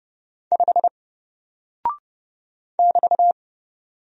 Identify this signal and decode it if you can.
Morse code, heard as a keyed tone, about 30 wpm, 700 Hz. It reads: 5 <BT>
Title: Single Character (Letter:Number:Punctuation:Prosign) Alt 30wpm